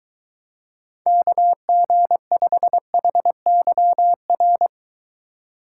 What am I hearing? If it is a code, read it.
Morse code, heard as a keyed tone, about 23 wpm, 700 Hz. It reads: KG5HYR